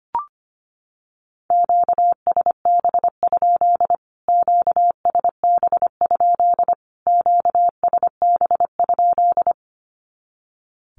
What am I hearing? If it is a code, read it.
Morse code, heard as a keyed tone, about 25 wpm, 700 Hz. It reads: QSB? QSB? QSB?
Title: Q-Codes Repeated 3x 25wpm